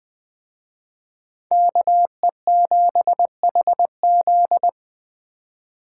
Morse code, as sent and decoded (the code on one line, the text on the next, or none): -.- . --... .... --..
KE7HZ